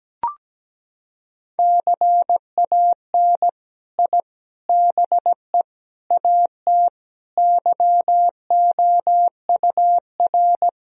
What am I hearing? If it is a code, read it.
Morse code, heard as a keyed tone, about 17 wpm, 700 Hz. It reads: CAN I BE AT YOUR
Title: Sentences from Top 1000 Words - Review Part 1 - 17wpm